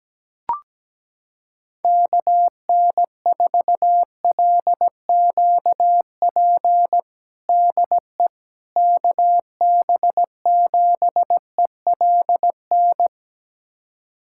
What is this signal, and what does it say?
Morse code, heard as a keyed tone, about 17 wpm, 700 Hz. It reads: KN4LQP DE KB7ELN
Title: USA Callsign DE Callsign 17wpm